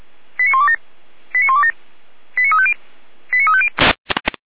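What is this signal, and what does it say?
Five-tone selective call (ZVEI), heard as quick runs of five tones, tones of about 70 ms. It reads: 89127 89127 89350 89350
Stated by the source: ZVEI five-tone selective call 89582 received on an SDR, fire brigade (Feuerwehr) Orlamünde.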